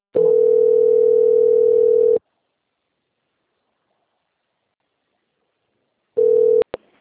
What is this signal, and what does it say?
Ringback tone (North America), 440 + 480 Hz.
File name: out- - - - - .28893